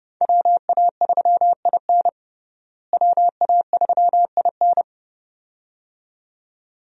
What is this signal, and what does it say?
Morse code, heard as a keyed tone, about 29 wpm, 700 Hz. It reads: WA3IN WA3IN
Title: USA Call Signs - 5 Characters - Repeated 2x - 30wpm